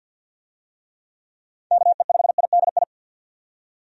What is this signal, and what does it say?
Morse code, heard as a keyed tone, about 52 wpm, 700 Hz. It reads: KE5IDI
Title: USA Call Signs - 6 Characters - 50wpm